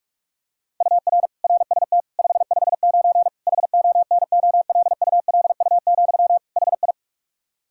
Morse code, heard as a keyed tone, about 46 wpm, 700 Hz. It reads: UR RST 559 HONOLULU, HI
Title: Practice QSO - Exchange 1 - Beginning - 45wpm